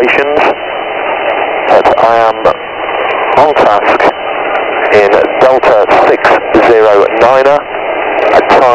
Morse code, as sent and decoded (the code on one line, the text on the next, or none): none